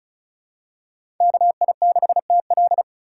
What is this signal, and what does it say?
Morse code, heard as a keyed tone, about 36 wpm, 700 Hz. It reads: KI6TL
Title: USA Call Signs - 4x Character Spacing - 35wpm